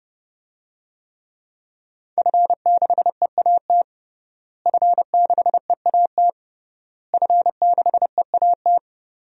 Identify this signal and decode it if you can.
Morse code, heard as a keyed tone, about 30 wpm, 700 Hz. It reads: F6EAT F6EAT F6EAT
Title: DXCC-100 Call Signs - 5 Characters - Repeated 3x - 30wpm